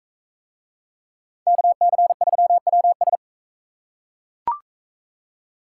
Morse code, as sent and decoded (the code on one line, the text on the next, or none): -.- -.-. ...-- .-- ...
KC3WS